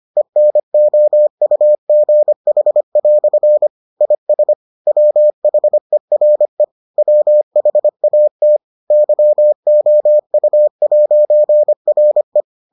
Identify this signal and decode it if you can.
Morse code, heard as a keyed tone, about 25 wpm, 600 Hz. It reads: ENOUGH" IS WHERE WHAT YOU'RE